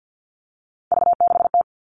Morse code, heard as a keyed tone, about 50 wpm, 700 Hz. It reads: V6T
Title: All Cut Numbers - 3-Digit 50wpm